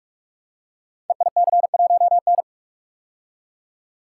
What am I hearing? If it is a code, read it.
Morse code, heard as a keyed tone, about 44 wpm, 690 Hz. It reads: EIC1N
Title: Code Groups - 5 Letter:Numbers - 45wpm